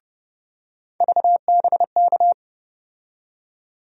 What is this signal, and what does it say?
Morse code, heard as a keyed tone, about 30 wpm, 700 Hz. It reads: VBK
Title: USA Call Sign Postfixes - 3 Letters 30wpm